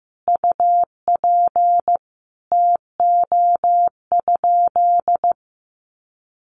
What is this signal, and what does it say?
Morse code, heard as a keyed tone, about 15 wpm, 700 Hz. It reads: UP TO?
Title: Sentences from Top 100 Words 15wpm